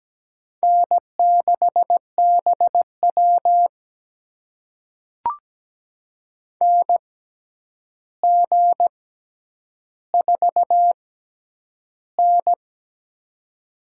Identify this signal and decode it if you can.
Morse code, heard as a keyed tone, about 17 wpm, spaced as for 9 wpm, 700 Hz. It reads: N6BW NG4N
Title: USA Call Signs - 4 Characters - 6x Character Spacing - 17wpm